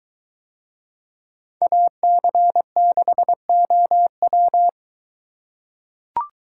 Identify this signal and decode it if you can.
Morse code, heard as a keyed tone, about 21 wpm, 700 Hz. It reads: AC6OW